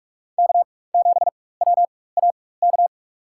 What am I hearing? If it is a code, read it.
Morse code, heard as a keyed tone, about 41 wpm, spaced as for 20 wpm, 700 Hz. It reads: K7WAK
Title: USA Call Signs - 5 Characters - 4x Character Spacing - 45wpm